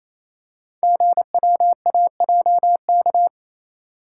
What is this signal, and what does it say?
Morse code, heard as a keyed tone, about 27 wpm, 700 Hz. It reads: GWAJK
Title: Code Groups - 5 Letter:Numbers - 28wpm